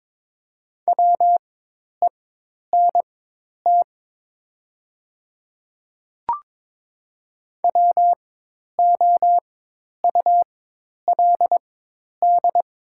Morse code, heard as a keyed tone, about 23 wpm, spaced as for 10 wpm, 700 Hz. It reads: WENT WOULD
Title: 4x Character Spacing - Top 500 Words - 22wpm